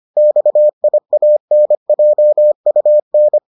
Morse code, heard as a keyed tone, about 25 wpm, 600 Hz. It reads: XIANJUN